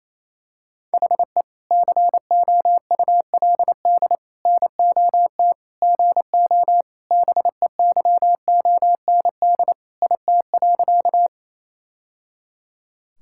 Morse code, heard as a keyed tone, about 28 wpm, 700 Hz. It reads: HE COULD NOT GO BEYOND IT.
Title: Sentences from Top 1000 Words - Review Part 6 - 28wpm